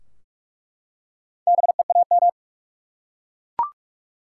Morse code, heard as a keyed tone, about 47 wpm, 700 Hz. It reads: BEAM
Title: Top 90 QSO Elements 45wpm